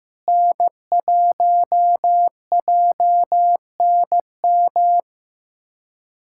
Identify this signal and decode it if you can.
Morse code, heard as a keyed tone, about 15 wpm, 700 Hz. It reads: N1JNM
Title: USA Callsign DE Callsign - 5 Characters - 15wpm